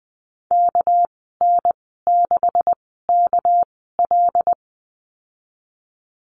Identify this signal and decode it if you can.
Morse code, heard as a keyed tone, about 19 wpm, spaced as for 14 wpm, 700 Hz. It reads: KN6KL
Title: 213 Lesson 34g - Number 8 - Call Signs 20wpm